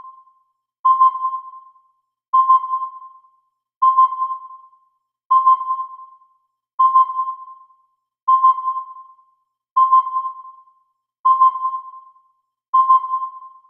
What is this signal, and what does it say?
There is no code.